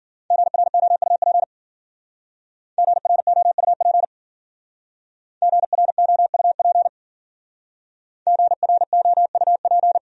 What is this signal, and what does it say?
Morse code, heard as a keyed tone, about 51 wpm, 700 Hz. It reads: GROUP GROUP GROUP GROUP